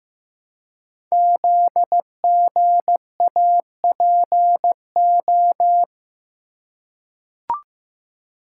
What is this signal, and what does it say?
Morse code, heard as a keyed tone, about 15 wpm, 700 Hz. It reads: ZGAPO
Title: Code Groups - 5 Letters - 15wpm